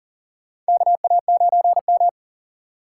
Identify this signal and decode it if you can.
Morse code, heard as a keyed tone, about 40 wpm, 700 Hz. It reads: KA9M
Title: USA Call Signs - 4 Characters - 6x Character Spacing - 40wpm